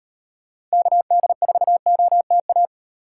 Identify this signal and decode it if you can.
Morse code, heard as a keyed tone, about 38 wpm, 700 Hz. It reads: KD4OTA